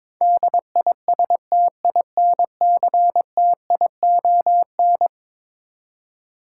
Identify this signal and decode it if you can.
Morse code, heard as a keyed tone, about 22 wpm, 700 Hz. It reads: DISTINCTION
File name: practice / 22wpm / Eleven Letter Words - 22wpm